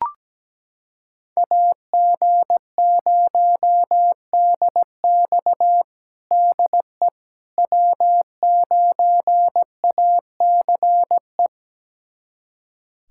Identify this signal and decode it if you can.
Morse code, heard as a keyed tone, about 17 wpm, 700 Hz. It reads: AG0DX DE W9ACE
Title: USA Callsign DE Callsign - 5 Characters - 17wpm